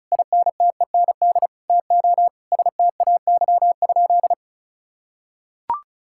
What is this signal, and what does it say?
Morse code, heard as a keyed tone, about 34 wpm, 700 Hz. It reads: INTEND TO STAY?
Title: Sentences from Top 1600 Words 35wpm